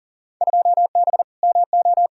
Morse code, heard as a keyed tone, about 40 wpm, 700 Hz. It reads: 2B MO
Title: Field Day Exchanges - 40wpm